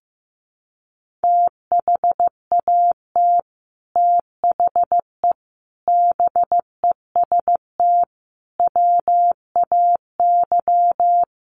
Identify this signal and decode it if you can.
Morse code, heard as a keyed tone, about 15 wpm, 700 Hz. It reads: THAT THE BEST WAY
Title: Sets of 5 Words Encore2 15wpm